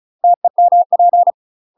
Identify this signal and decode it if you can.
Morse code, heard as a keyed tone, about 34 wpm, 700 Hz. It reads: TEMP
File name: UK MASTER QSOs TRAINING/35WPM